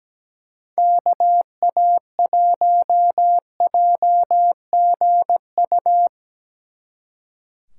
Morse code, heard as a keyed tone, about 18 wpm, 700 Hz. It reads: KA1JGU
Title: Speed Racing - USA 2x3 Call Signs 17wpm